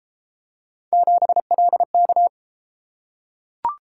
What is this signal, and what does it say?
Morse code, heard as a keyed tone, about 32 wpm, 700 Hz. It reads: 7LK